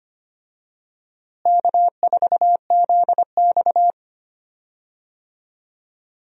Morse code, heard as a keyed tone, about 25 wpm, 700 Hz. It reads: K4ZX
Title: USA Call Signs - 4 Characters - Repeated 3x - 25wpm